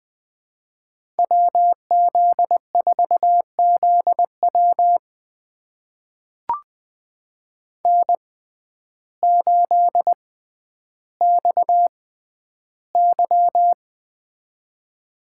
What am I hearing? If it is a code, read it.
Morse code, heard as a keyed tone, about 21 wpm, spaced as for 11 wpm, 700 Hz. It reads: WZ4ZW N8XY
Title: USA Call Signs - 6x Character Spacing - 20wpm